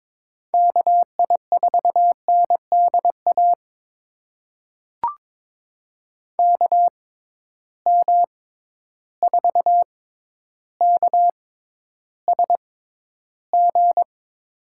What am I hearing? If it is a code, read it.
Morse code, heard as a keyed tone, about 22 wpm, spaced as for 11 wpm, 700 Hz. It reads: KI4NDA KM4KSG
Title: USA Call Signs - 6 Characters - 6x Character Spacing - 22wpm